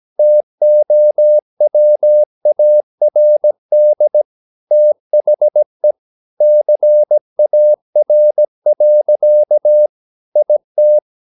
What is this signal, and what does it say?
Morse code, heard as a keyed tone, about 17 wpm, 600 Hz. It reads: TOWARD THE CAR. IT
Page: Romanen Never af Ken Follet på engelsk.